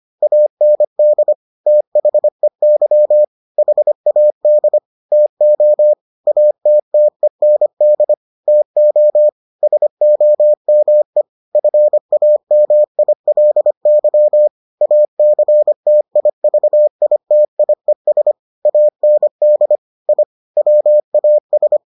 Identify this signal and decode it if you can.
Morse code, heard as a keyed tone, about 25 wpm, 600 Hz. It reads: AND THEY HAD TO ATTEND TO SOME FAMILY ACTIVITIES AND I WAS